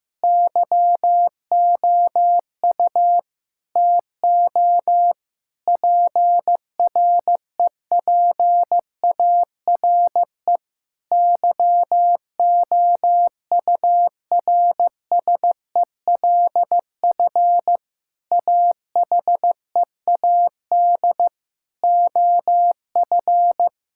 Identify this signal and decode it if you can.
Morse code, heard as a keyed tone, about 15 wpm, 700 Hz. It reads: YOU TO PREPARE YOURSELF AHEAD OF
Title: Sentences from Top 1000 Words - Review Part 6 - 15wpm